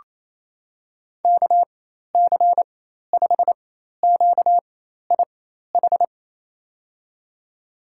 Morse code, heard as a keyed tone, about 28 wpm, spaced as for 13 wpm, 700 Hz. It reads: KC5QIH
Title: USA Call Signs - 6 Characters - 4x Character Spacing - 28wpm